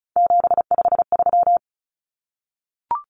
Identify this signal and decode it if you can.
Morse code, heard as a keyed tone, about 34 wpm, 700 Hz. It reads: 753
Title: Common Cut Numbers - 3-Digit 35wpm